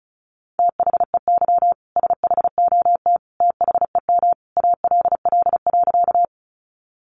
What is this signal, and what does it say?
Morse code, heard as a keyed tone, about 34 wpm, 700 Hz. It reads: THEY SHOT THEM ALL.
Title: Sentences from Top 1100 Words 35wpm